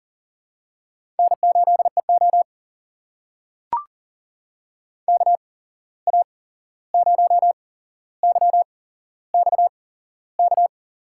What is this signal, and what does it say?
Morse code, heard as a keyed tone, about 43 wpm, spaced as for 15 wpm, 700 Hz. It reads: N8EO KA0YXK
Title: USA Call Signs - 8x Character Spacing - 40wpm